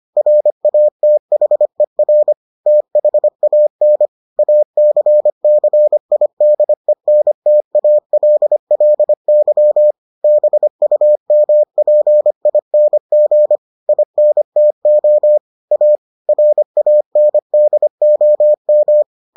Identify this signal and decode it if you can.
Morse code, heard as a keyed tone, about 25 wpm, 600 Hz. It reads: RATHER THAN ACCIDENTALLY BUMPING INTO A RANDOM